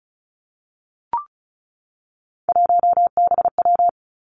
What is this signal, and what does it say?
Morse code, heard as a keyed tone, about 36 wpm, 700 Hz. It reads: 1BW
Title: Three Challenging Letter-Number Combinations 35wpm